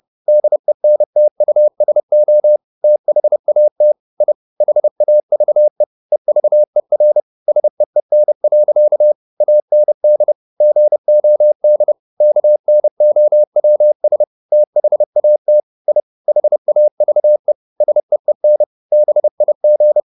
Morse code, heard as a keyed tone, about 30 wpm, 600 Hz. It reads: DENTUSO THAT I HAVE EVER SEEN. AND GOD KNOWS THAT I HAVE SEEN BIG